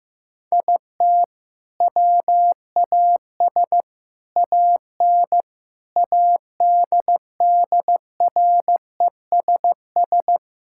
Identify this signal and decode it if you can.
Morse code, heard as a keyed tone, about 15 wpm, 700 Hz. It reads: IT WAS AN ADDRESS
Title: Sentences from Top 1000 Words - Review Part 6 - 15wpm